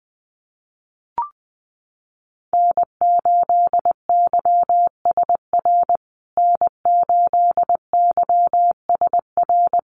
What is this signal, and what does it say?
Morse code, heard as a keyed tone, about 20 wpm, 700 Hz. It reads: N8YSR N8YSR
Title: USA Call Signs Repeated 3x 20wpm